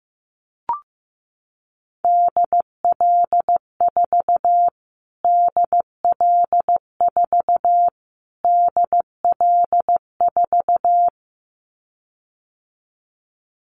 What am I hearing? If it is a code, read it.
Morse code, heard as a keyed tone, about 15 wpm, 700 Hz. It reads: DL4 DL4 DL4
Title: Callsign Prefixes for Top 25 DXCC - Repeated 3x - 15wpm